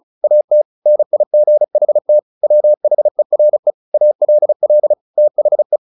Morse code, heard as a keyed tone, about 34 wpm, 600 Hz. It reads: AT NIGHT WHERE ALL THE
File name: the old man and the sea / the old man and the sea 1 35wpm